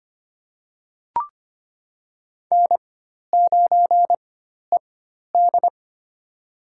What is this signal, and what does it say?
Morse code, heard as a keyed tone, about 25 wpm, spaced as for 12 wpm, 700 Hz. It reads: N9ED
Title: USA Call Signs - 4x Character Spacing - 25wpm